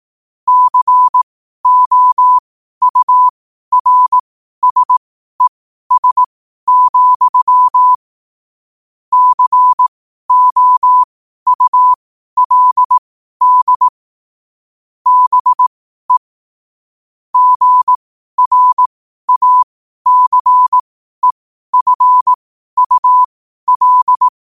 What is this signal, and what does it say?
Morse code, heard as a keyed tone, about 18 wpm, spaced as for 12 wpm, 1000 Hz. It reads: COURSES, COULD BE GRACEFUL
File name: qotdfiles / 12wpm